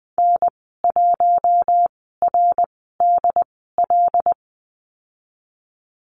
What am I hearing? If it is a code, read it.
Morse code, heard as a keyed tone, about 20 wpm, spaced as for 14 wpm, 700 Hz. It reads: N1RDL
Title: 154 Lesson 25g - Letter p - Call Signs 20wpm